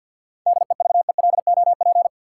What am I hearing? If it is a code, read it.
Morse code, heard as a keyed tone, about 53 wpm, 700 Hz. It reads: DEVELOP
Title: Sentences from Top 1000 Words - Review Part 2 - New Word Drill - 50wpm